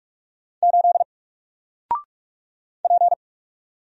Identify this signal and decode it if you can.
Morse code, heard as a keyed tone, about 44 wpm, 700 Hz. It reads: 8 P
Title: Single Letter-Number Alt 45wpm